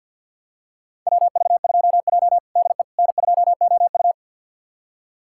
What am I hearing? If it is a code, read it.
Morse code, heard as a keyed tone, about 50 wpm, 700 Hz. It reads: WV2J DE N2OU
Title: USA Callsign DE Callsign - 4 Characters - 50wpm